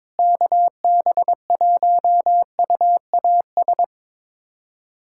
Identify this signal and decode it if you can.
Morse code, heard as a keyed tone, about 22 wpm, 700 Hz. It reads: KB1UAS
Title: USA Callsign DE Callsign - 6 Characters - 22wpm